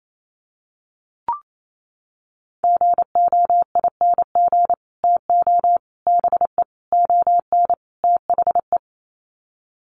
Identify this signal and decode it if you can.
Morse code, heard as a keyed tone, about 28 wpm, 700 Hz. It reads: GOING TO BE ON THE
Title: Sets of 5 Words Encore2 28wpm